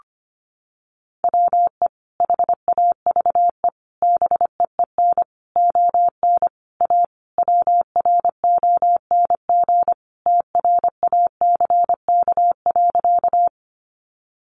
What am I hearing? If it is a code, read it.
Morse code, heard as a keyed tone, about 25 wpm, 700 Hz. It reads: WE HAVE BEEN ON A WRONG TRACK.
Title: Sentences from Top 1100 Words 25wpm